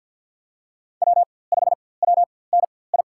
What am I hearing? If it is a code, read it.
Morse code, heard as a keyed tone, about 50 wpm, spaced as for 23 wpm, 700 Hz. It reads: W5WNI
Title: USA Call Signs - 5 Characters - 4x Character Spacing - 50wpm